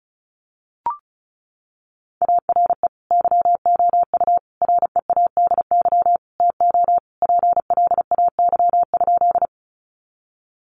Morse code, heard as a keyed tone, about 34 wpm, 700 Hz. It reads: ARE YOU READY TO PLAY?
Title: Sentences from Top 400 Words 35wpm